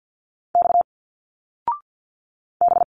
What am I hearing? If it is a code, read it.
Morse code, heard as a keyed tone, about 50 wpm, 700 Hz. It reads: X B
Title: Single Letters 50wpm